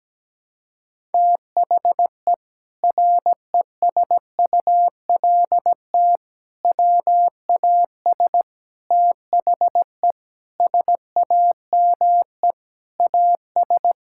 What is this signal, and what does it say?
Morse code, heard as a keyed tone, about 17 wpm, 700 Hz. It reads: THE RESULT WAS THE SAME AS
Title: Sentences from Top 1000 Words - Review Part 5 - 17wpm